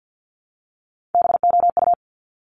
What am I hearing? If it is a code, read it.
Morse code, heard as a keyed tone, about 50 wpm, 700 Hz. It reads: BOU
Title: Top Three Letter Combinations in Words 50wpm